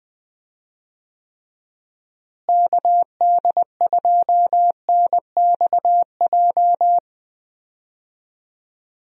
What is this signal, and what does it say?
Morse code, heard as a keyed tone, about 20 wpm, 700 Hz. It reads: KD2NXJ
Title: USA Call Signs - 6 Characters - Repeated 2x - 20wpm